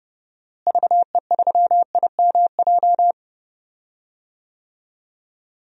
Morse code, heard as a keyed tone, about 30 wpm, 700 Hz. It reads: VE3IMJ
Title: DXCC-100 Call Signs - 6 Characters - Repeated 2x - 30wpm